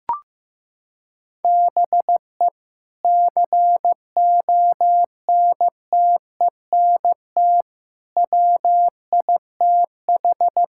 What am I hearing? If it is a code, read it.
Morse code, heard as a keyed tone, about 15 wpm, 700 Hz. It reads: BE CONTENT WITH
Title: Sentences from Top 1000 Words - Review Part 6 - 15wpm